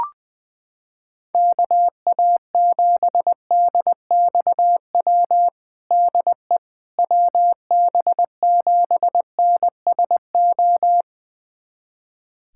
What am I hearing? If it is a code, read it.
Morse code, heard as a keyed tone, about 21 wpm, 700 Hz. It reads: KA7DXW DE WB7NSO
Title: USA Callsign DE Callsign 20wpm